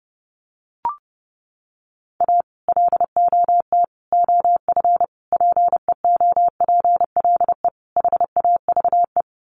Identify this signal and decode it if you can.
Morse code, heard as a keyed tone, about 30 wpm, 700 Hz. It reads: A LOT OF PEOPLE HAVE
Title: Sets of 5 Words 30wpm